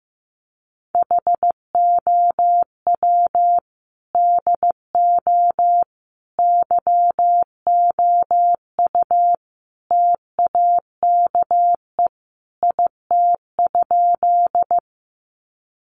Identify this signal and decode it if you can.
Morse code, heard as a keyed tone, about 15 wpm, 700 Hz. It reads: HOW DO YOU TAKE IT?
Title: Sentences from Top 200 Words 15wpm